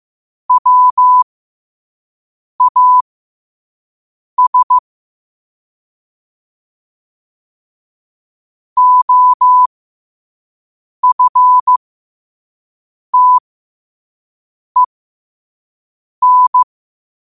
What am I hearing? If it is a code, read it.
Morse code, heard as a keyed tone, about 15 wpm, spaced as for 5 wpm, 1000 Hz. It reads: WAS OFTEN
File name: qotdfiles / 5wpm